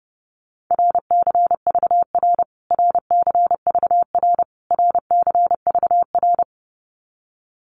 Morse code, heard as a keyed tone, about 30 wpm, 700 Hz. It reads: RCVR RCVR RCVR
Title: Abbreviations Repeated 3x - 30wpm